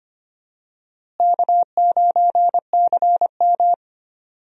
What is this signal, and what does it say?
Morse code, heard as a keyed tone, about 25 wpm, 700 Hz. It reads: K9CM
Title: USA Call Signs - 4 Characters - Repeated 2x - 25wpm